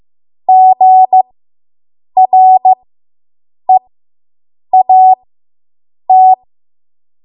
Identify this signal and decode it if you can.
Morse code, heard as a keyed tone, about 15 wpm, spaced as for 7 wpm, 750 Hz. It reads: GREAT